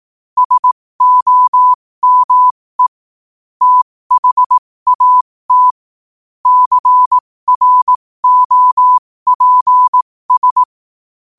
Morse code, heard as a keyed tone, about 18 wpm, spaced as for 15 wpm, 1000 Hz. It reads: SOME THAT CROPS
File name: qotdfiles / 15wpm